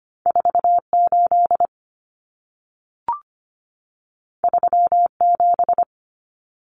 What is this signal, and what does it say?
Morse code, heard as a keyed tone, about 25 wpm, 700 Hz. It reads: 48 37
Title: Numbers - 2-Digit 25wpm